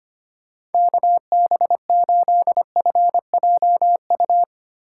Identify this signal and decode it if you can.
Morse code, heard as a keyed tone, about 32 wpm, 700 Hz. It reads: KB8FJU